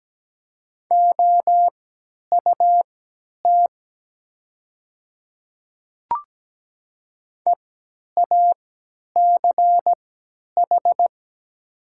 Morse code, heard as a keyed tone, about 17 wpm, spaced as for 10 wpm, 700 Hz. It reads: OUT EACH
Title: 3x Character Spacing - Top 300 Words - 17wpm